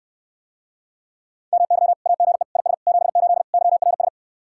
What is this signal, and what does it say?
Morse code, heard as a keyed tone, about 69 wpm, 700 Hz. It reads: MY AGE IS 78 YRS